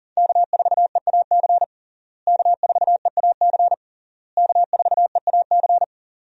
Morse code, heard as a keyed tone, about 40 wpm, 700 Hz. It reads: K4EAC K4EAC K4EAC
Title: USA Call Signs - 5 Characters - Repeated 3x - 40wpm